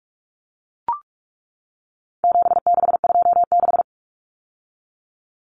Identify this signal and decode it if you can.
Morse code, heard as a keyed tone, about 44 wpm, 700 Hz. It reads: 7626
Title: Numbers - 4-Digit 45wpm